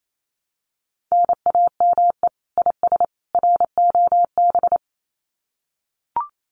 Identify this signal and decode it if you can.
Morse code, heard as a keyed tone, about 28 wpm, 700 Hz. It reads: NAME IS ROB
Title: Common QSO Phrases 28wpm